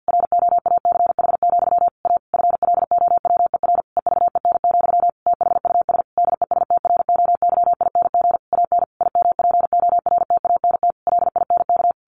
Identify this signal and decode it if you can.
Morse code, heard as a keyed tone, about 50 wpm, 700 Hz. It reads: ROACH, A FLOWER EVEN, THUS DESTROYING AN IMPORTANT LINK
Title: A Sound Of Thunder 50wpm